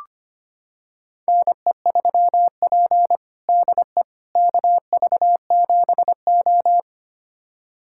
Morse code, heard as a keyed tone, about 25 wpm, 700 Hz. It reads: NE3P DE KV7O
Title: USA Callsign DE Callsign - 4 Characters - 25wpm